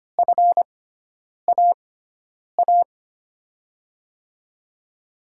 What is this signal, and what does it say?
Morse code, heard as a keyed tone, about 25 wpm, spaced as for 9 wpm, 700 Hz. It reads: FAA
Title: USA Call Signs - 6 Characters - 6x Character Spacing - 25wpm